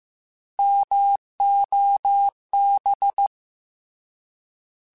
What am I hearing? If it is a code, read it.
Morse code, heard as a keyed tone, about 15 wpm, 800 Hz. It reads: MOB
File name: Alphabetical / Alphabetical mp3s / Ms @13